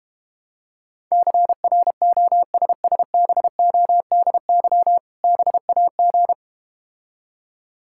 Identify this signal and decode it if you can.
Morse code, heard as a keyed tone, about 32 wpm, 700 Hz. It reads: CROSSBODY BAG